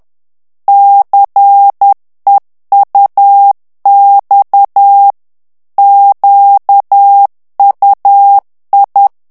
Morse code, heard as a keyed tone, about 10 wpm, 800 Hz. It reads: CEUX QUI
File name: FRANCAIS TEXTE CLAIR 2 VITESSE 10 MOTS MINUTES